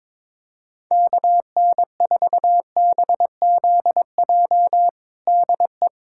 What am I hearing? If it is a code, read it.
Morse code, heard as a keyed tone, about 22 wpm, 700 Hz. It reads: KN4BZJ DE
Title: USA Callsign DE Callsign - 6 Characters - 22wpm